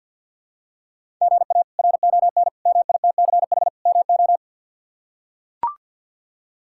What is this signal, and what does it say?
Morse code, heard as a keyed tone, about 45 wpm, 700 Hz. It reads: GA RON MITCH MO
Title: K1USN SST QSO Practice - 50wpm